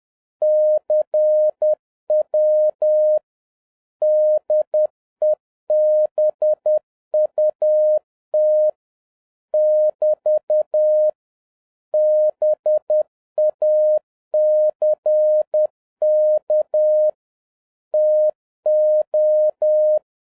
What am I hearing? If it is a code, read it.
Morse code, heard as a keyed tone, about 10 wpm, 610 Hz. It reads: CW DEBUT <BT> BACK TO